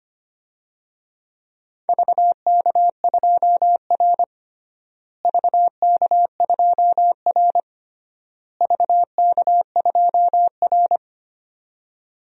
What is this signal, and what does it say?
Morse code, heard as a keyed tone, about 25 wpm, 700 Hz. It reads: VK2R VK2R VK2R
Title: DXCC-100 Call Signs - 4 Characters - Repeated 3x - 25wpm